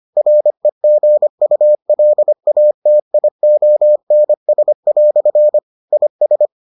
Morse code, heard as a keyed tone, about 25 wpm, 600 Hz. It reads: REGULATIONS" IS